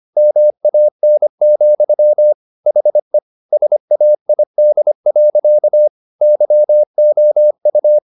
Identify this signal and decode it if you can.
Morse code, heard as a keyed tone, about 25 wpm, 600 Hz. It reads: MAN, HE SAID. YOU